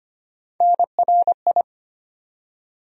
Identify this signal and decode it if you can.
Morse code, heard as a keyed tone, about 25 wpm, 700 Hz. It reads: NRI
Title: USA Call Sign Postfixes - 3 Letters 25wpm